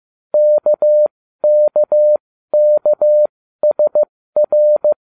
Morse code, heard as a keyed tone, about 15 wpm, spaced as for 13 wpm, 600 Hz. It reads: KKKSR